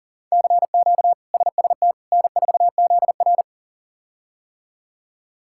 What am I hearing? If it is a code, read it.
Morse code, heard as a keyed tone, about 40 wpm, 700 Hz. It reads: CQ SST N4ZR
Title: K1USN SST QSO Practice - 40wpm